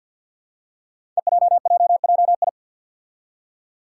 Morse code, heard as a keyed tone, about 47 wpm, 700 Hz. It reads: EJJJI
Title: Five Challenging Letter-Number Combinations 50wpm